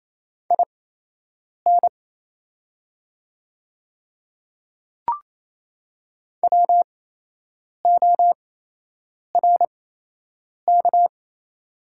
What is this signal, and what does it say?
Morse code, heard as a keyed tone, about 27 wpm, spaced as for 8 wpm, 690 Hz. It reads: IN WORK
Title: 8x Character Spacing - Top 300 Words - 28wpm